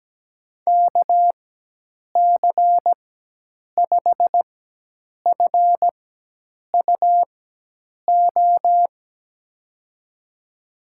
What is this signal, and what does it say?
Morse code, heard as a keyed tone, about 17 wpm, spaced as for 8 wpm, 700 Hz. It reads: KC5FUO
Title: USA Call Signs - 4x Character Spacing - 17wpm